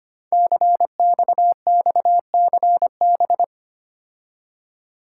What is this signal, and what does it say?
Morse code, heard as a keyed tone, about 25 wpm, 700 Hz. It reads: CXXCB
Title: Code Groups - 5 Letters - 25wpm